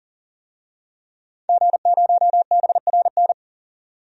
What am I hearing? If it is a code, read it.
Morse code, heard as a keyed tone, about 40 wpm, 700 Hz. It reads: G0BRN
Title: DXCC-100 Call Signs - 5 Characters - 40wpm